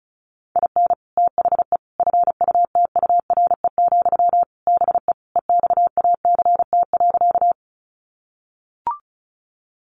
Morse code, heard as a keyed tone, about 34 wpm, 700 Hz. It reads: IN THE FUTURE, BE EXACT.
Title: Sentences from Top 1300 Words 35wpm